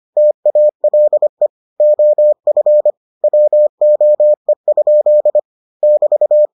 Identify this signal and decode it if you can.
Morse code, heard as a keyed tone, about 25 wpm, 600 Hz. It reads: TALE OF WOE? <BT>